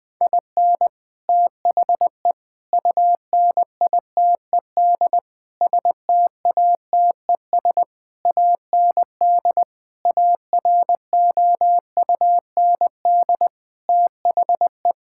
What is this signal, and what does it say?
Morse code, heard as a keyed tone, about 20 wpm, 700 Hz. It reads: IN THE UNITED STATES AND AROUND THE
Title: Sets of 8 Words - 20wpm